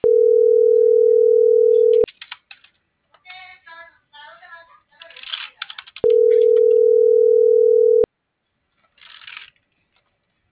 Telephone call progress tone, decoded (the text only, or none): ringback tone